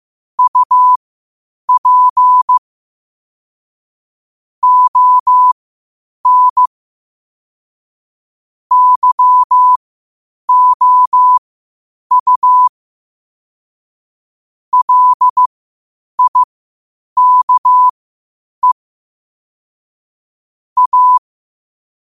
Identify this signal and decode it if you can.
Morse code, heard as a keyed tone, about 15 wpm, spaced as for 8 wpm, 1000 Hz. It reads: UP ON YOU LIKE A
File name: qotdfiles / 8wpm